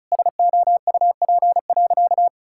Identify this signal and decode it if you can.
Morse code, heard as a keyed tone, about 36 wpm, 700 Hz. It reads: SOUP.
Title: Sentences from Top 1000 Words - Review Part 6 - 35wpm